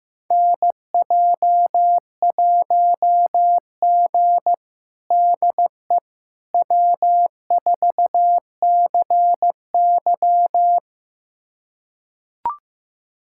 Morse code, heard as a keyed tone, about 15 wpm, 700 Hz. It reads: NJ1G DE W4CY
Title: USA Callsign DE Callsign - 4 Characters - 15wpm